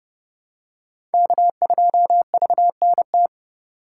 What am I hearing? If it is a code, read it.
Morse code, heard as a keyed tone, about 29 wpm, 700 Hz. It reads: K2VNT